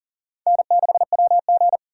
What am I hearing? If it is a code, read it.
Morse code, heard as a keyed tone, about 40 wpm, 700 Hz. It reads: N6WG
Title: K1USN SST QSO Practice - 40wpm